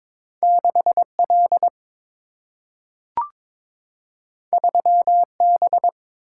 Morse code, heard as a keyed tone, about 22 wpm, 700 Hz. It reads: 6 L 3 B
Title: Single-Letter Single-Number 22wpm